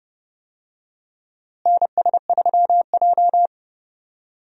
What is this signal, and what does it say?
Morse code, heard as a keyed tone, about 30 wpm, 700 Hz. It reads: NS3J
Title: USA Call Signs - 4 Characters - Repeated 3x - 30wpm